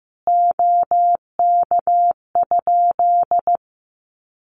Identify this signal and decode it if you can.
Morse code, heard as a keyed tone, about 15 wpm, 700 Hz. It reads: OK?
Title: Common QSO Phrases 15wpm